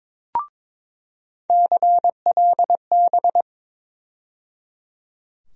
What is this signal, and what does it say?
Morse code, heard as a keyed tone, about 22 wpm, 700 Hz. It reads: CLB
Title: USA Call Sign Postfixes - 3 Letters 22wpm